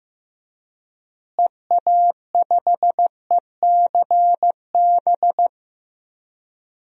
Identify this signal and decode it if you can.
Morse code, heard as a keyed tone, about 15 wpm, 700 Hz. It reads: EA5ECB
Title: DXCC-100 Call Signs - 6 Characters - Repeated 3x - 15wpm